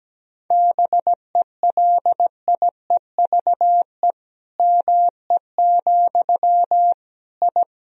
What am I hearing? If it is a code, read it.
Morse code, heard as a keyed tone, about 17 wpm, 700 Hz. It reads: BELIEVE ME, I
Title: Sentences from Top 1000 Words - Review Part 2 - 17wpm